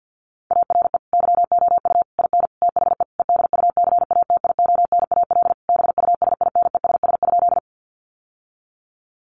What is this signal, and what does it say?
Morse code, heard as a keyed tone, about 50 wpm, 700 Hz. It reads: ARE YOU IN THE EDUCATIONAL BUSINESS?
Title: Sentences from Top 1200 Words 50wpm